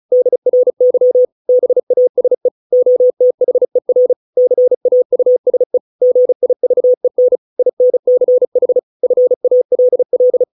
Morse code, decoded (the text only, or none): DRY BASE OTHER CAUSE GIVEN INCH FALL